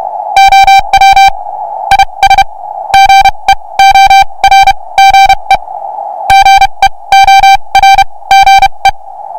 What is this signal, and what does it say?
Morse code, heard as a keyed tone, about 32 wpm, spaced as for 24 wpm, 2350 Hz, with noise. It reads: OW IS GEORGE GEORGE